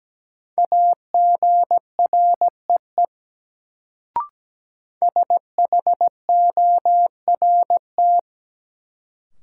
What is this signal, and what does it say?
Morse code, heard as a keyed tone, about 17 wpm, 700 Hz. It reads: AGREE SHORT
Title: Sets of 5 Words Encore - New Word Drill - 17wpm